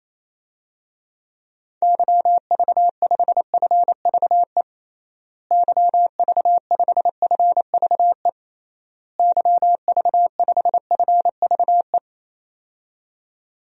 Morse code, heard as a keyed tone, about 27 wpm, 700 Hz. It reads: YV5FVE YV5FVE YV5FVE
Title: DXCC-100 Call Signs - 6 Characters - Repeated 3x - 28wpm